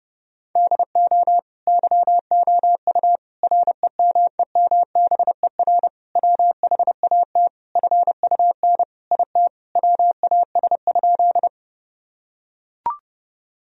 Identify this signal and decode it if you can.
Morse code, heard as a keyed tone, about 30 wpm, 700 Hz. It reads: DO YOU REMEMBER WHAT FUN IT WAS?
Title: Sentences from Top 1000 Words - Review Part 4 - 30wpm